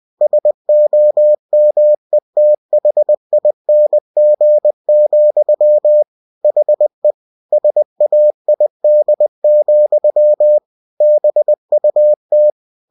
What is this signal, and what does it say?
Morse code, heard as a keyed tone, about 20 wpm, 600 Hz. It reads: SOMETHING, HE SAID, BUT
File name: cw-books / winnie-the-pooh / winnie-20wpm-page-0032